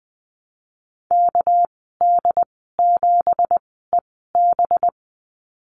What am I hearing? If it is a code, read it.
Morse code, heard as a keyed tone, about 20 wpm, spaced as for 14 wpm, 700 Hz. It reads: KD7EB
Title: 238 Lesson 38g - Letter q - Call Signs 20wpm